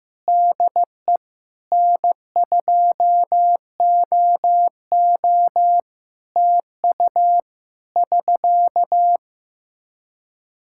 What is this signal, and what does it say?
Morse code, heard as a keyed tone, about 16 wpm, 700 Hz. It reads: DE N2OO TU <SK>
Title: K1USN SST QSO Practice 15wpm